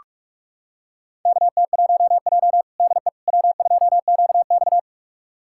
Morse code, heard as a keyed tone, about 47 wpm, 700 Hz. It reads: KT1J DE W2QX
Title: USA Callsign DE Callsign - 4 Characters - 45wpm